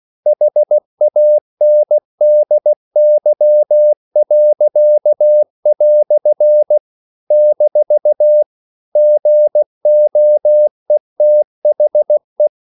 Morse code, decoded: HANDY." - GOETHE